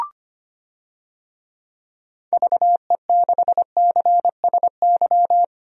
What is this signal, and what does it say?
Morse code, heard as a keyed tone, about 25 wpm, 700 Hz. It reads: VE6CSY
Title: DXCC-100 Call Signs - 6 Characters - 25wpm